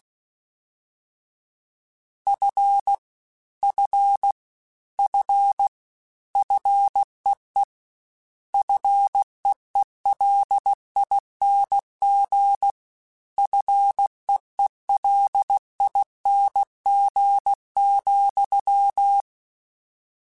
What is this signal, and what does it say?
Morse code, heard as a keyed tone, about 16 wpm, 800 Hz. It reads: F F F FEE FEELING FEELING,
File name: Short Runs / Short Runs-mp3s / SinglePrefixes @15